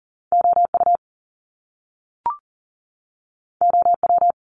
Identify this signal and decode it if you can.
Morse code, heard as a keyed tone, about 38 wpm, 700 Hz. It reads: OU OW
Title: Top Two Letter Combinations in Words 40wpm